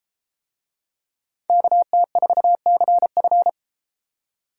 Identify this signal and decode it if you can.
Morse code, heard as a keyed tone, about 32 wpm, 700 Hz. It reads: KT4CF